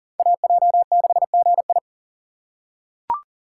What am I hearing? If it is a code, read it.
Morse code, heard as a keyed tone, about 40 wpm, 700 Hz. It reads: AJ6GI
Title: USA Call Signs - 5 Characters - 6x Character Spacing - 40wpm